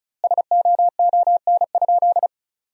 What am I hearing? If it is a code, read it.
Morse code, heard as a keyed tone, about 34 wpm, 700 Hz. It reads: SOON?
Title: 2x Word-Spacing - Sentences from Top 400 Words - 35wpm